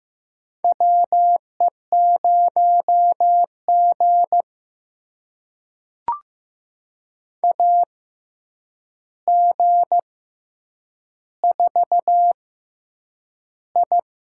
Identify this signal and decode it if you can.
Morse code, heard as a keyed tone, about 15 wpm, spaced as for 8 wpm, 700 Hz. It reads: WE0G AG4I
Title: USA Call Signs - 4 Characters - 6x Character Spacing - 15wpm